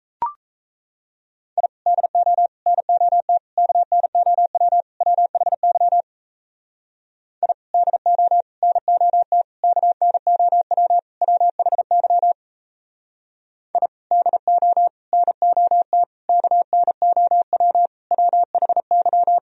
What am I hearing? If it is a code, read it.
Morse code, heard as a keyed tone, about 34 wpm, 700 Hz. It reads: I DO NOT KNOW WHY I DO NOT KNOW WHY I DO NOT KNOW WHY
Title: Speed-Racing - Sets of 5 Words 28wpm